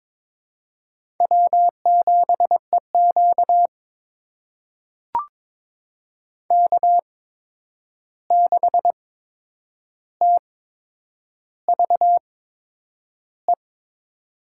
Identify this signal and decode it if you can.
Morse code, heard as a keyed tone, about 22 wpm, spaced as for 9 wpm, 700 Hz. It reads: W7EQ K6TVE
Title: USA Call Signs - 8x Character Spacing - 22wpm